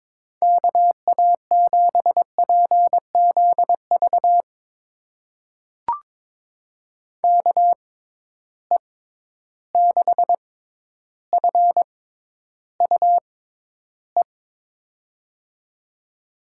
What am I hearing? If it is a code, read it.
Morse code, heard as a keyed tone, about 22 wpm, spaced as for 11 wpm, 700 Hz. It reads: KA7PZV KE6FUE
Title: USA Call Signs - 6 Characters - 6x Character Spacing - 22wpm